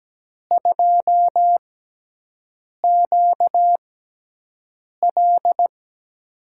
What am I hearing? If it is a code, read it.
Morse code, heard as a keyed tone, about 17 wpm, spaced as for 6 wpm, 700 Hz. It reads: 2QL
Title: USA Call Signs - 4 Characters - 6x Character Spacing - 17wpm